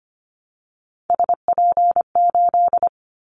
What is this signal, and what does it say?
Morse code, heard as a keyed tone, about 25 wpm, 700 Hz. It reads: SP8
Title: Callsign Prefixes for Top 25 DXCC - Repeated 3x - 25wpm